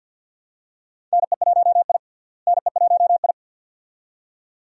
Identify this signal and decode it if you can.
Morse code, heard as a keyed tone, about 49 wpm, 700 Hz. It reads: NE1I NE1I
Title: USA Call Signs - 4 Characters - Repeated 2x - 50wpm